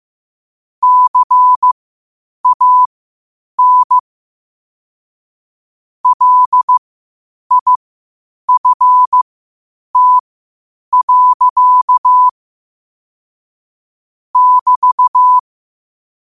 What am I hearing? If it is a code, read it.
Morse code, heard as a keyed tone, about 15 wpm, spaced as for 8 wpm, 1000 Hz. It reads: CAN LIFT. <BT>
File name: qotdfiles / 8wpm